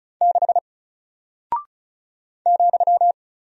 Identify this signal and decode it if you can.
Morse code, heard as a keyed tone, about 36 wpm, 700 Hz. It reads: 6 ,
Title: Single Character (Letter:Number:Punctuation:Prosign) Alt 35wpm